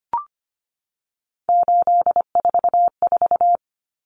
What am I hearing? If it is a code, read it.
Morse code, heard as a keyed tone, about 25 wpm, 700 Hz. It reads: 844
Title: Common Cut Numbers - 3-Digit 25wpm